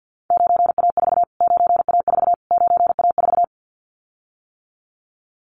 Morse code, heard as a keyed tone, about 49 wpm, 700 Hz. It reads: 9A4 9A4 9A4
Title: Callsign Prefixes for Top 25 DXCC - Repeated 3x - 50wpm